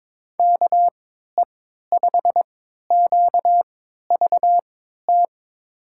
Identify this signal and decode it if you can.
Morse code, heard as a keyed tone, about 22 wpm, spaced as for 12 wpm, 700 Hz. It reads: KE5QVT
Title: USA Call Signs - 6 Characters - 3x Character Spacing - 22wpm